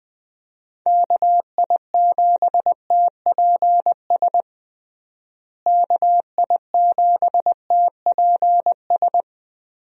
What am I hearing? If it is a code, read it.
Morse code, heard as a keyed tone, about 20 wpm, 700 Hz. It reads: KI7TPS KI7TPS
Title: USA Call Signs - 6 Characters - Repeated 2x - 20wpm